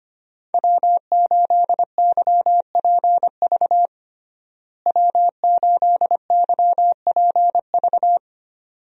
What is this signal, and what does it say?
Morse code, heard as a keyed tone, about 25 wpm, 700 Hz. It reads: W8YPV W8YPV
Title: USA Call Signs - 5 Characters - Repeated 2x - 25wpm